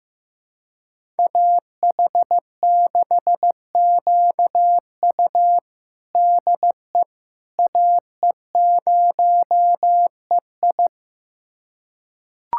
Morse code, heard as a keyed tone, about 15 wpm, 700 Hz. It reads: AH6QU DE AE0EI
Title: USA Callsign DE Callsign - 5 Characters - 15wpm